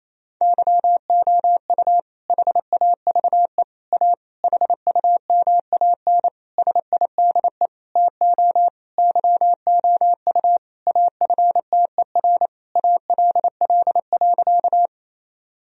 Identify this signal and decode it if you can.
Morse code, heard as a keyed tone, about 28 wpm, 690 Hz. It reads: YOU HAVE A HUMAN SIDE TO YOU AFTER ALL.
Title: Sentences from Top 1000 Words - Review Part 3 - 28wpm